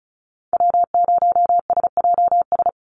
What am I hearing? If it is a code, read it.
Morse code, heard as a keyed tone, about 34 wpm, 700 Hz. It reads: W0SJS
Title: Call Signs - 35wpm